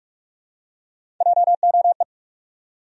Morse code, heard as a keyed tone, about 46 wpm, 700 Hz. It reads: JOE
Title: USA Common Names - Max 5 Letters - 45wpm